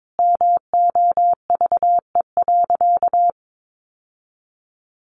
Morse code, heard as a keyed tone, about 22 wpm, 700 Hz. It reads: MOVE.
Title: Sentences from Top 1200 Words 22wpm